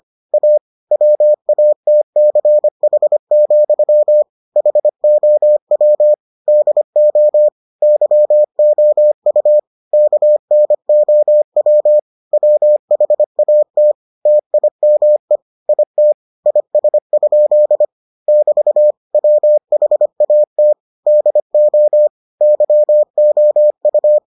Morse code, heard as a keyed tone, about 25 wpm, 600 Hz. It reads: A WATCH, HOW DO YOU KNOW WHAT TIME IT IS? <BT> WHAT DO YOU